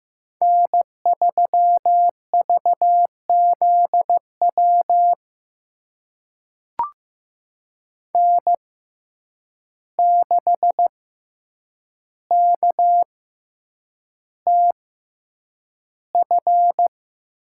Morse code, heard as a keyed tone, about 15 wpm, spaced as for 8 wpm, 700 Hz. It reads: N3VZW N6KTF
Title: USA Call Signs - 5 Characters - 6x Character Spacing - 15wpm